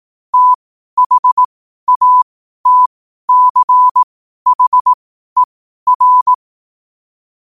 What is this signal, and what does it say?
Morse code, heard as a keyed tone, about 18 wpm, spaced as for 12 wpm, 1000 Hz. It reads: THATCHER
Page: New quotes every day in morse code at 12 Words per minute.